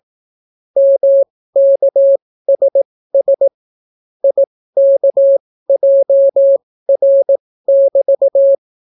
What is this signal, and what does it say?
Morse code, heard as a keyed tone, about 18 wpm, spaced as for 14 wpm, 550 Hz. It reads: MKSS IKJR<BT>